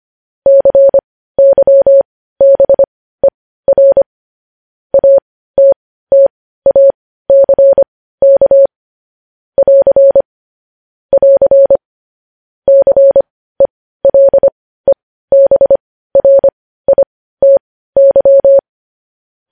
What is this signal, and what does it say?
Morse code, heard as a keyed tone, about 25 wpm, spaced as for 15 wpm, 550 Hz. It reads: CYBER ATTACK <AR> <AR> CELEBRITY
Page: News headlines at 15